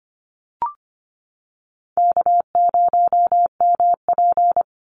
Morse code, heard as a keyed tone, about 24 wpm, 700 Hz. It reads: K0MP
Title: NA SOTA Chaser Callsigns 25wpm